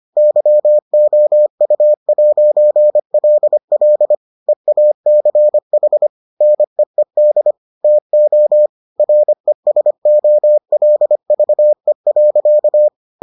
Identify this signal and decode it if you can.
Morse code, heard as a keyed tone, about 25 wpm, 600 Hz. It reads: YOU'LL EACH NEED TO RESOLVE.